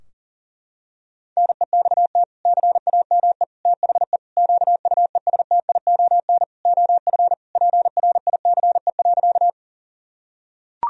Morse code, heard as a keyed tone, about 40 wpm, 700 Hz. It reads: NEXT CAME THE QUESTION OF PRICE.
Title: Sentences from Top 1000 Words - Review Part 4 - 40wpm